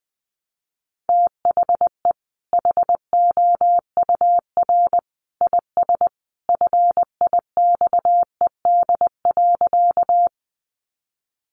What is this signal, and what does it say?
Morse code, heard as a keyed tone, about 20 wpm, 700 Hz. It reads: THE HOUR IS FIXED.
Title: Sentences from Top 1100 Words 20wpm